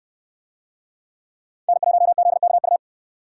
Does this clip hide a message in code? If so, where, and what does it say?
Yes: Morse code from 1.7 to 2.8 s, heard as a keyed tone, about 68 wpm, 700 Hz. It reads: N1ZKU